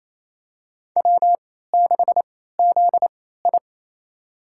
Morse code, heard as a keyed tone, about 28 wpm, spaced as for 16 wpm, 700 Hz. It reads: W6ZI
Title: USA Call Signs - 4 Characters - 3x Character Spacing - 28wpm